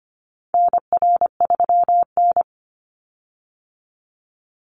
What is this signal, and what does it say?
Morse code, heard as a keyed tone, about 25 wpm, 700 Hz. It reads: NR3N
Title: Call Signs - 25wpm